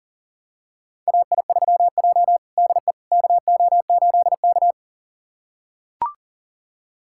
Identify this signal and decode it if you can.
Morse code, heard as a keyed tone, about 40 wpm, 700 Hz. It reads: AI3J DE KO8K
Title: USA Callsign DE Callsign - 4 Characters - 40wpm